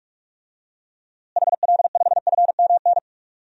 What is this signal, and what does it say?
Morse code, heard as a keyed tone, about 46 wpm, 700 Hz. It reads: HL5FMN
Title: DXCC-100 Call Signs - 6 Characters - Repeated 2x - 45wpm